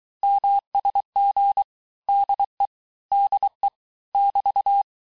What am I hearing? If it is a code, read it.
Morse code, heard as a keyed tone, about 23 wpm, 800 Hz. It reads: MSG DE DE <BT>
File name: teste abreviations vitesse 17mpm puis 22mpm